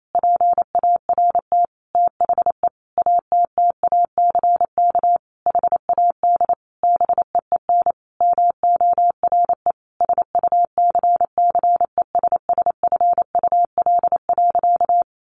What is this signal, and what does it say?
Morse code, heard as a keyed tone, about 28 wpm, 700 Hz. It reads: PART THE ATTACK HAD BEEN MORE SUCCESSFUL.
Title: Sentences from Top 1500 Words 28wpm